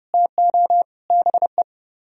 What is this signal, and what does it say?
Morse code, heard as a keyed tone, about 32 wpm, 700 Hz. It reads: TO BE